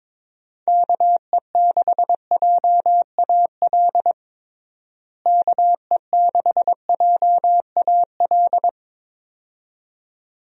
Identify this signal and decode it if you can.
Morse code, heard as a keyed tone, about 22 wpm, 700 Hz. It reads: KE6JAL KE6JAL
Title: USA Call Signs - 6 Characters - Repeated 2x - 22wpm